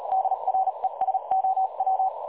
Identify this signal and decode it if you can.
Morse code, heard as a keyed tone, about 38 wpm, spaced as for 33 wpm, 760 Hz. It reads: DRESOW
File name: dlazaawansowanych / nagraniazpasma / websdr recording 3565.0kHz